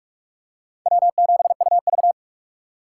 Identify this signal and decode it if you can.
Morse code, heard as a keyed tone, about 40 wpm, 700 Hz. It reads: W7UV